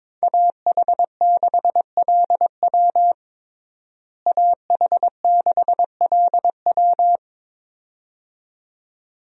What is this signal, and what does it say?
Morse code, heard as a keyed tone, about 22 wpm, 700 Hz. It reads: AH6LW AH6LW
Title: USA Call Signs - 5 Characters - Repeated 2x - 22wpm